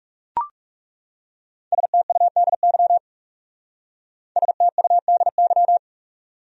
Morse code, heard as a keyed tone, about 37 wpm, 700 Hz. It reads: STUDY STUDY
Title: Speed-Racing Top 300 Words 30wpm